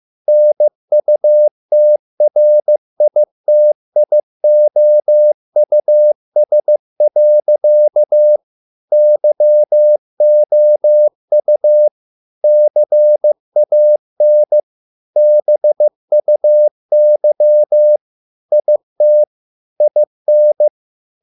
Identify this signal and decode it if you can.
Morse code, heard as a keyed tone, about 15 wpm, 600 Hz. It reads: NUTRITIOUS. YOU CAN BUY IT IN